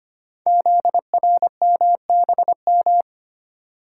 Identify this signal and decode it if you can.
Morse code, heard as a keyed tone, about 25 wpm, 700 Hz. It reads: ZRMBM
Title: Code Groups - 5 Letters - 25wpm